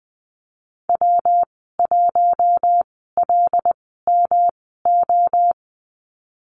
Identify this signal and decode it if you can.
Morse code, heard as a keyed tone, about 20 wpm, spaced as for 15 wpm, 700 Hz. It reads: W1LMO
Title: 154 Lesson 25g - Letter p - Call Signs 20wpm